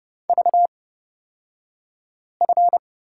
Morse code, heard as a keyed tone, about 31 wpm, spaced as for 5 wpm, 690 Hz. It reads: VF
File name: Lesson 36d - BT - Two Characters 30wpm